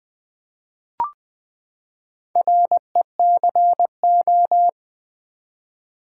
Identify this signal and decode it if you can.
Morse code, heard as a keyed tone, about 20 wpm, 700 Hz. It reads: RECO
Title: Build A Word - 4 Letters At A Time - 11-Letter Words - 20wpm